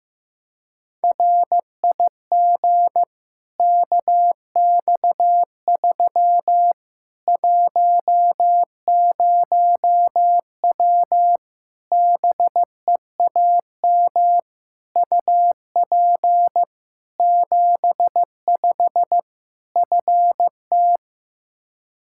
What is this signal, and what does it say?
Morse code, heard as a keyed tone, about 15 wpm, 700 Hz. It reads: RIG KX3 10W BEAM UP 75 FT
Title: Practice QSO - Exchange 2 - Info - 15wpm